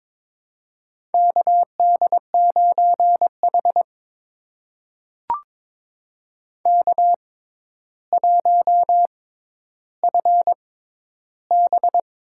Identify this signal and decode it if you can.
Morse code, heard as a keyed tone, about 21 wpm, spaced as for 11 wpm, 700 Hz. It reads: KD9H K1FB
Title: USA Call Signs - 4 Characters - 6x Character Spacing - 22wpm